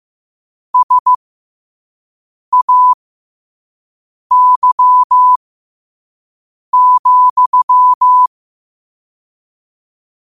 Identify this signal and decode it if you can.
Morse code, heard as a keyed tone, about 15 wpm, spaced as for 5 wpm, 1000 Hz. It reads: SAY,
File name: qotdfiles / 5wpm